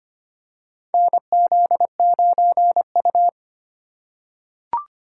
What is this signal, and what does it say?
Morse code, heard as a keyed tone, about 25 wpm, 700 Hz. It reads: NZ9U
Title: USA Call Signs - 4 Characters - 4x Character Spacing - 25wpm